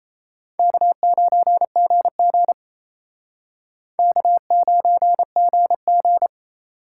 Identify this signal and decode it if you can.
Morse code, heard as a keyed tone, about 34 wpm, 700 Hz. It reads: K9GG K9GG
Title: Speed Racing - USA 1x2 Call Signs 28wpm